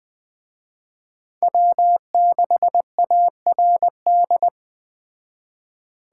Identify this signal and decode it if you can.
Morse code, heard as a keyed tone, about 20 wpm, 700 Hz. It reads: W6ARD